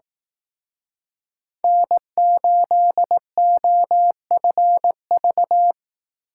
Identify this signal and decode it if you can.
Morse code, heard as a keyed tone, about 18 wpm, 700 Hz. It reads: N8OFV
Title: Speed Racing - USA 1x3 Call Signs 15wpm